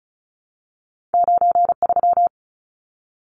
Morse code, heard as a keyed tone, about 36 wpm, 700 Hz. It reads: 93
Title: Numbers - 2-Digit 35wpm